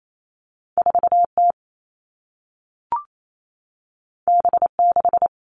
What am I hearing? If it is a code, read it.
Morse code, heard as a keyed tone, about 28 wpm, 700 Hz. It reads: 4T B6
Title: All Cut Numbers - 2-Digit 28wpm